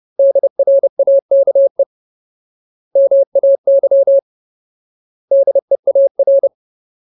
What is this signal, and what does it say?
Morse code, heard as a keyed tone, about 30 wpm, 550 Hz. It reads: DRAKE MAY DEAR
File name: Ultimate Word List / 30WPM Ultimate